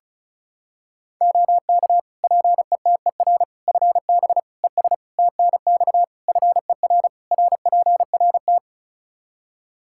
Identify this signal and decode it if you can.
Morse code, heard as a keyed tone, about 34 wpm, 700 Hz. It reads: OK PETER FB ES TNX FER RPRT
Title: K1USN SST QSO Practice 35wpm